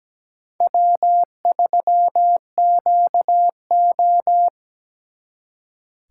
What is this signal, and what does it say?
Morse code, heard as a keyed tone, about 22 wpm, 700 Hz. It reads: W3QO